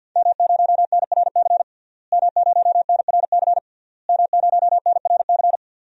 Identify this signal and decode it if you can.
Morse code, heard as a keyed tone, about 50 wpm, 700 Hz. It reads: M0NRC M0NRC M0NRC
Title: DXCC-100 Call Signs - 5 Characters - Repeated 3x - 50wpm